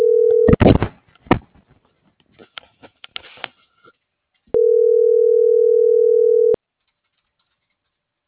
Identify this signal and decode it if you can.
Ringback tone (North America), 440 + 480 Hz.